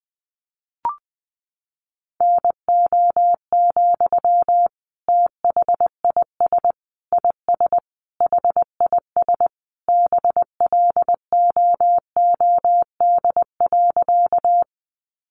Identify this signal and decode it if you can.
Morse code, heard as a keyed tone, about 20 wpm, 700 Hz. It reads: NO, THIS IS HIS BLOOD.
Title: Sentences from Top 1100 Words 20wpm